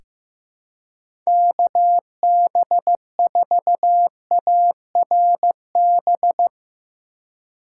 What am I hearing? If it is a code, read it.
Morse code, heard as a keyed tone, about 15 wpm, 700 Hz. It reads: KB4ARB
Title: USA Call Signs - 6 Characters - 2x Character Spacing - 15wpm